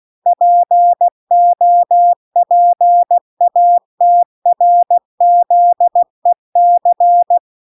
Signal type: Morse code